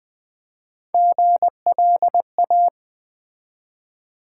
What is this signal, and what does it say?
Morse code, heard as a keyed tone, about 20 wpm, 700 Hz. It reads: GLA
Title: Build A Word - 1 Letter At A Time - 3 to 5-Letter Words - 20wpm